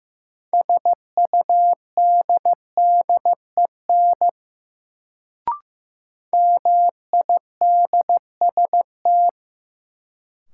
Morse code, heard as a keyed tone, about 15 wpm, 700 Hz. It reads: SUDDEN MIDST
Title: Sets of 4 Words - New Word Drill - 15wpm